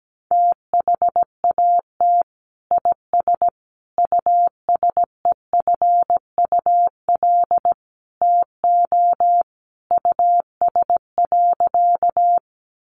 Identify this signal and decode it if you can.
Morse code, heard as a keyed tone, about 17 wpm, 700 Hz. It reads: THAT IS USEFUL TO US.
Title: Sentences from Top 700 Words 17wpm